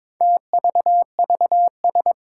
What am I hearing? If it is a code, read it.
Morse code, heard as a keyed tone, about 22 wpm, 700 Hz. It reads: TVVS
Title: Four Challenging Letter-Number Combinations 22wpm